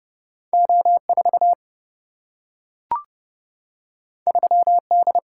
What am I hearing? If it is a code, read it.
Morse code, heard as a keyed tone, about 30 wpm, 700 Hz. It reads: O 4 3 D
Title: Single-Letter Single-Number 30wpm